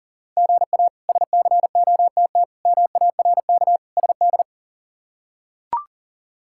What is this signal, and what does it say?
Morse code, heard as a keyed tone, about 40 wpm, 700 Hz. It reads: GA SCOTT MARK SD
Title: K1USN SST QSO Practice - 40wpm